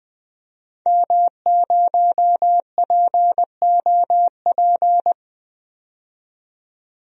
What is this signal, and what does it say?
Morse code, heard as a keyed tone, about 20 wpm, 700 Hz. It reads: M0POP
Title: DXCC-100 Call Signs - 5 Characters - 20wpm